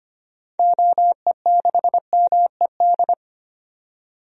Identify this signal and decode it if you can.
Morse code, heard as a keyed tone, about 25 wpm, 700 Hz. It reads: OE6MED
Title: DXCC-100 Call Signs - 6 Characters - 25wpm